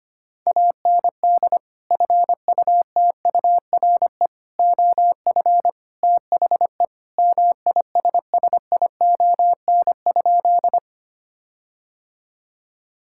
Morse code, heard as a keyed tone, about 25 wpm, 700 Hz. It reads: AND FUTURE OF THE MISSION?
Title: Sentences from Top 1600 Words 25wpm